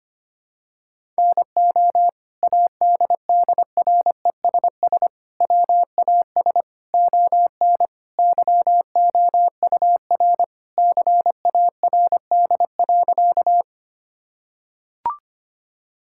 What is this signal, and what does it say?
Morse code, heard as a keyed tone, about 25 wpm, 700 Hz. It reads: NO ADDRESS WAS ON YOUR CARD.
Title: Sentences from Top 1000 Words - Review Part 6 - 25wpm